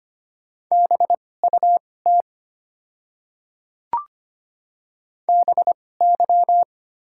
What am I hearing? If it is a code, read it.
Morse code, heard as a keyed tone, about 25 wpm, spaced as for 18 wpm, 700 Hz. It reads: BUT BY
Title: 2x Character Spacing - Top 100 Words - 25wpm